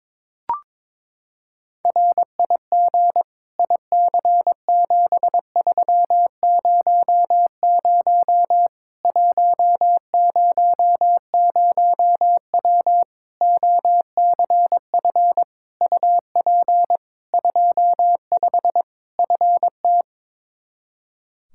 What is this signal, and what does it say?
Morse code, heard as a keyed tone, about 21 wpm, 700 Hz. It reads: RIG IC7300 100W OCF UP 25 FT
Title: Practice QSO - Exchange 2 - Info - 22wpm